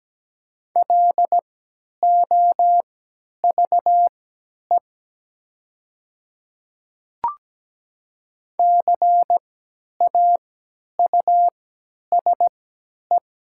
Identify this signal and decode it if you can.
Morse code, heard as a keyed tone, about 17 wpm, spaced as for 10 wpm, 690 Hz. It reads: LOVE CAUSE
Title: 3x Character Spacing - Top 500 Words - 17wpm